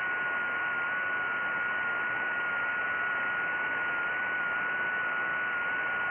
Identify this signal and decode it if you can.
FT8.
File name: IC-7100